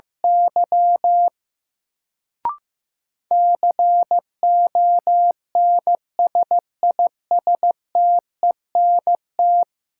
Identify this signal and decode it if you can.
Morse code, heard as a keyed tone, about 16 wpm, 700 Hz. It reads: Y CONSISTENT
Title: Sentences from Top 1500 Words - New Word Drill - 15wpm